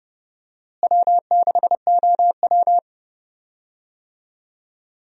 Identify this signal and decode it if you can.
Morse code, heard as a keyed tone, about 30 wpm, 700 Hz. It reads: W6OW
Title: USA Call Signs - 4 Characters - Repeated 3x - 30wpm